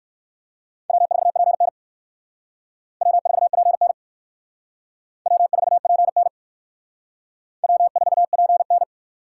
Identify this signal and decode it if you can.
Morse code, heard as a keyed tone, about 56 wpm, 700 Hz. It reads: W4PN W4PN W4PN W4PN
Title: Speed Racing - USA 1x2 Call Signs 45wpm